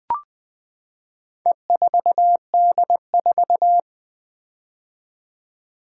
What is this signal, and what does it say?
Morse code, heard as a keyed tone, about 20 wpm, 700 Hz. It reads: E4D4
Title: Four Letter-Number Combinations - 20wpm